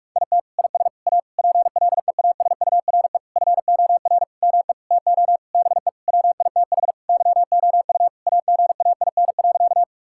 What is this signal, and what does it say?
Morse code, heard as a keyed tone, about 46 wpm, 700 Hz. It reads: IT IS A PLEASURE FOR ME TO BE WITH YOU AGAIN.
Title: Sentences from Top 1600 Words 45wpm